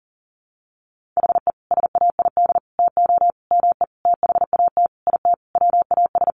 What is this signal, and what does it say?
Morse code, heard as a keyed tone, about 40 wpm, 700 Hz. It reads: HE SAID TO ME THAT IT WAS
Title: Sentences from Top 1500 Words 40wpm